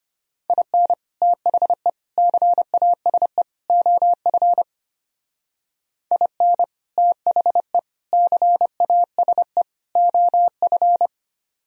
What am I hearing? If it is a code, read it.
Morse code, heard as a keyed tone, about 33 wpm, 700 Hz. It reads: IN THE CASE OF IN THE CASE OF